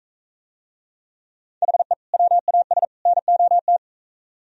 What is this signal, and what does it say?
Morse code, heard as a keyed tone, about 43 wpm, 700 Hz. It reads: HE WAS NOT